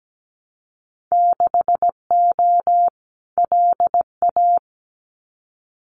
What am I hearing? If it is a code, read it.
Morse code, heard as a keyed tone, about 17 wpm, 700 Hz. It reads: 6O LA
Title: Winter Field Day Exchanges Encore 17wpm